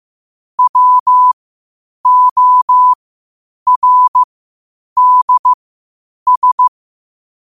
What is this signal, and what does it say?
Morse code, heard as a keyed tone, about 15 wpm, spaced as for 8 wpm, 1000 Hz. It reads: WORDS
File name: qotdfiles / 8wpm